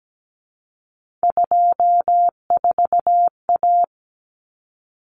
Morse code, heard as a keyed tone, about 17 wpm, 700 Hz. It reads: 24A
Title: Common Cut Numbers - 3-Digit 17wpm